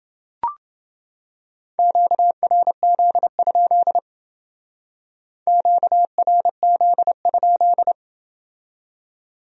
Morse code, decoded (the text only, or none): QRZ? QRZ?